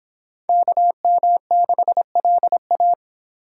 Morse code, heard as a keyed tone, about 30 wpm, 700 Hz. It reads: KM6LA